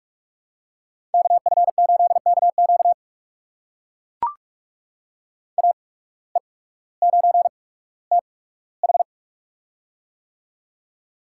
Morse code, heard as a keyed tone, about 44 wpm, spaced as for 19 wpm, 700 Hz. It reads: KF8KQ AE9TH
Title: USA Call Signs - 5 Characters - 8x Character Spacing - 45wpm